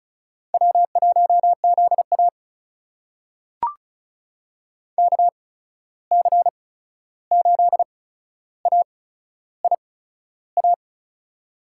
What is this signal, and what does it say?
Morse code, heard as a keyed tone, about 36 wpm, spaced as for 13 wpm, 700 Hz. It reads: W1ZA KC8AIA
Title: USA Call Signs - 8x Character Spacing - 35wpm